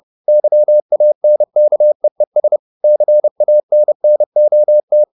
Morse code, heard as a keyed tone, about 30 wpm, 600 Hz. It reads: YANKEES CANNOT
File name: the old man and the sea / the old man and the sea 1 30wpm